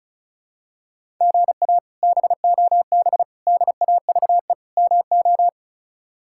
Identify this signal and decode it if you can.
Morse code, heard as a keyed tone, about 36 wpm, 700 Hz. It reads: GA BOB DAVE MO
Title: K1USN SST QSO Practice - 35wpm